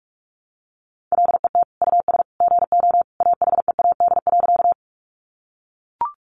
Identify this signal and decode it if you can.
Morse code, heard as a keyed tone, about 44 wpm, 700 Hz. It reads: LET US GO AHEAD.
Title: Sentences from Top 1100 Words 45wpm